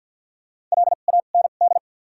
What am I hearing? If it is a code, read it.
Morse code, heard as a keyed tone, about 53 wpm, 700 Hz. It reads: L A N D
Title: 2x Character Spacing - Top 300 Words - 50wpm